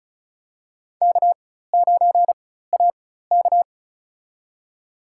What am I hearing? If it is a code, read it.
Morse code, heard as a keyed tone, about 36 wpm, spaced as for 16 wpm, 700 Hz. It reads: K9AK
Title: USA Call Signs - 4x Character Spacing - 35wpm